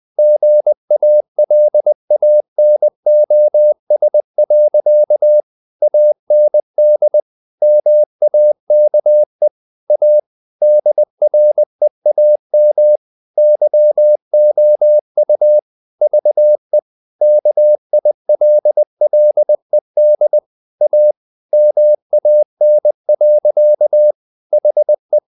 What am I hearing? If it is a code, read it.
Morse code, heard as a keyed tone, about 20 wpm, 600 Hz. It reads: GALANOS. AND MAKE A DREAM YOU VE KILLED A MAN. HE
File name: the old man and the sea part 2 20wpm